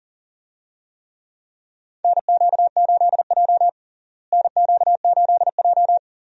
Morse code, heard as a keyed tone, about 37 wpm, 700 Hz. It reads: NQ8J NQ8J
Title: USA Call Signs - 4 Characters - Repeated 2x - 40wpm